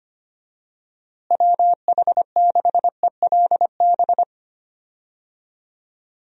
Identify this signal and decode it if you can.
Morse code, heard as a keyed tone, about 25 wpm, 700 Hz. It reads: WH6ELB
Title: USA Call Signs - 6 Characters - Repeated 3x - 25wpm